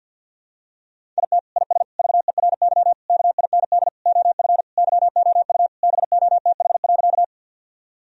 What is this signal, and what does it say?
Morse code, heard as a keyed tone, about 50 wpm, 700 Hz. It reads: IT IS VERY KIND OF YOU BOTH.
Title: Sentences from Top 1000 Words - Review Part 1 - 50wpm